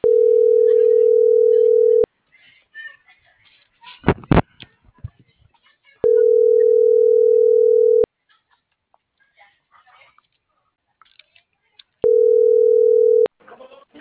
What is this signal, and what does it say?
Ringback tone (North America), 440 + 480 Hz.
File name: CENTRAL MULTIMODAL / out alfredovelazquez